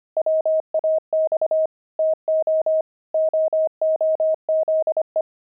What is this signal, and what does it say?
Morse code, heard as a keyed tone, about 25 wpm, 630 Hz. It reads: WAX TO OOZE
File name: pangram-fs-25-cs-25-fragment